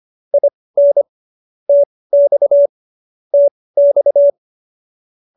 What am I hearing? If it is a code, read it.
Morse code, heard as a keyed tone, about 25 wpm, spaced as for 18 wpm, 570 Hz. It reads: IN TX TX